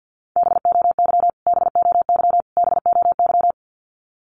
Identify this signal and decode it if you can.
Morse code, heard as a keyed tone, about 50 wpm, 700 Hz. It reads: BOY BOY BOY
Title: Top 200 Words Repeated 50wpm